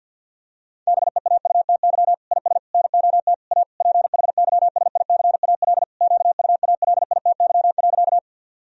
Morse code, heard as a keyed tone, about 51 wpm, 700 Hz. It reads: BEAUTY IS NOT A PHYSICAL QUALITY.
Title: Sentences from Top 1000 Words - Review Part 4 - 50wpm